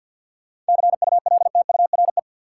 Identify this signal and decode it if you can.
Morse code, heard as a keyed tone, about 50 wpm, 700 Hz. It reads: CULTURE
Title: Sentences from Top 1000 Words - Review Part 6 - New Word Drill - 50wpm